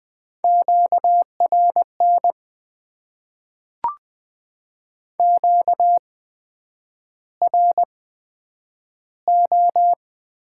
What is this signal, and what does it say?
Morse code, heard as a keyed tone, about 20 wpm, spaced as for 9 wpm, 700 Hz. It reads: QRN QRO
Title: Top 90 QSO Elements - 8x Character Spacing - 20wpm